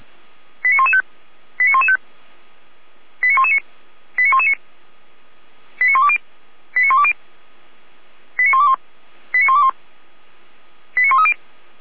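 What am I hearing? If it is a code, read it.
Five-tone selective call (ZVEI), heard as quick runs of five tones, tones of about 70 ms. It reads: 89115 89115 89119 89119 89120 89120 89121 89121 89130
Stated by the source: ZVEI five-tone selective call on 455 received by SDR.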